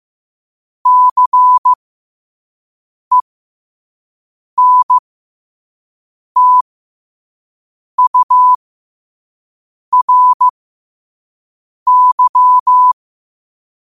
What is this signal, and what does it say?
Morse code, heard as a keyed tone, about 15 wpm, spaced as for 5 wpm, 1000 Hz. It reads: CENTURY
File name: qotdfiles / 5wpm